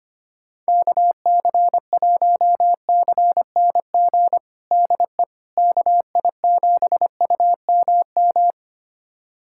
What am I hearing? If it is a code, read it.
Morse code, heard as a keyed tone, about 25 wpm, 700 Hz. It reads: KC1CNG DE KI7UMM
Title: USA Callsign DE Callsign 25wpm